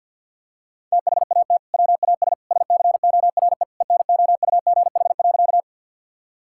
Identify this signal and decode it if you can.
Morse code, heard as a keyed tone, about 50 wpm, 700 Hz. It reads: THAT WAS SCORE ENOUGH.
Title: Sentences from Top 1600 Words 50wpm